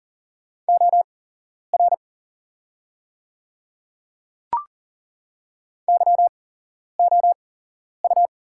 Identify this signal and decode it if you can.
Morse code, heard as a keyed tone, about 38 wpm, spaced as for 11 wpm, 690 Hz. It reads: OR YOU
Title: 8x Character Spacing - Top 200 Words - 40wpm